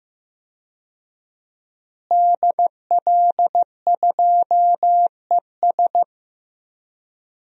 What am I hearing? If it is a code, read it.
Morse code, heard as a keyed tone, about 16 wpm, 700 Hz. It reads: DL2ES
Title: DXCC-100 Call Signs - 5 Characters - Repeated 2x - 15wpm